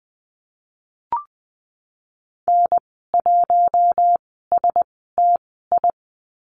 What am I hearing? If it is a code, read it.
Morse code, heard as a keyed tone, about 20 wpm, spaced as for 15 wpm, 700 Hz. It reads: N1STI
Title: 055 Lesson 10g - Letter r - Call Signs 20wpm